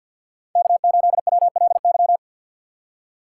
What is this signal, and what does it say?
Morse code, heard as a keyed tone, about 53 wpm, 700 Hz. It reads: K8WLY
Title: USA Call Signs - 5 Characters - 4x Character Spacing - 50wpm